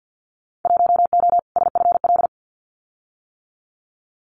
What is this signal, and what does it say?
Morse code, heard as a keyed tone, about 50 wpm, 700 Hz. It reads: 1O SFL
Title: Winter Field Day Exchanges Encore 50wpm